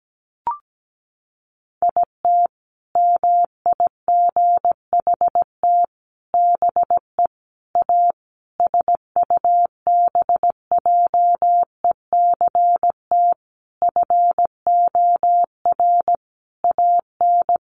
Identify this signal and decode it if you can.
Morse code, heard as a keyed tone, about 17 wpm, 700 Hz. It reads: IT MIGHT BE A SUBJECT FOR AN
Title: Sentences from Top 1500 Words 17wpm